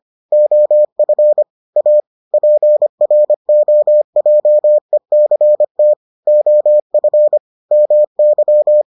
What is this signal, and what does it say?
Morse code, heard as a keyed tone, about 25 wpm, 600 Hz. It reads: OF A PROJECT OF MY